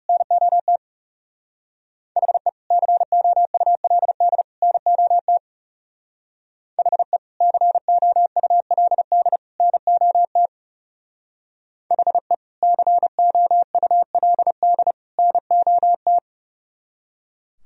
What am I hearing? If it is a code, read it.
Morse code, heard as a keyed tone, about 34 wpm, 700 Hz. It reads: NOT HE COULD NOT HE COULD NOT HE COULD NOT
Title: Speed-Racing - Sets of 3 Words 30wpm